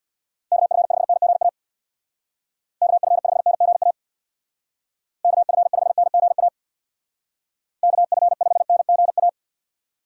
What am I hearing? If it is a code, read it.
Morse code, heard as a keyed tone, about 60 wpm, 690 Hz. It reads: KF5NGA KF5NGA KF5NGA KF5NGA
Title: Speed Racing - USA 2x3 Call Signs 50wpm